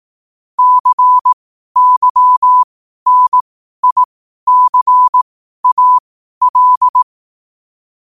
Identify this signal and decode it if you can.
Morse code, heard as a keyed tone, about 18 wpm, spaced as for 12 wpm, 1000 Hz. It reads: CYNICAL
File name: qotdfiles / 12wpm